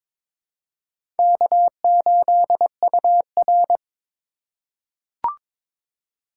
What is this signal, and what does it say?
Morse code, heard as a keyed tone, about 23 wpm, 700 Hz. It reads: K8UR
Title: USA Call Signs - 4x Character Spacing - 22wpm